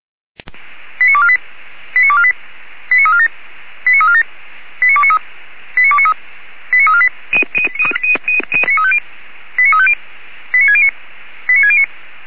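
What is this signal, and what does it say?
Five-tone selective call (ZVEI), heard as quick runs of five tones, tones of about 70 ms. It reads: 89248 89248 89347 89347 89293 89293 89348 89350 89609 89609
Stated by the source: ZVEI five-tone selective call received by SDR.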